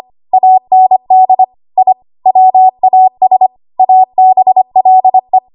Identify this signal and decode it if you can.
Morse code, heard as a keyed tone, about 25 wpm, 750 Hz. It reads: AND I WAS ABLE
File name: Archive / 25 WPM / 25WPM